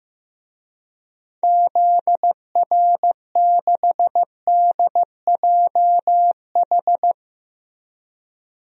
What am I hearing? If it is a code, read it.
Morse code, heard as a keyed tone, about 15 wpm, 700 Hz. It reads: ZR6DJH
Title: DXCC-100 Call Signs - 6 Characters - Repeated 2x - 15wpm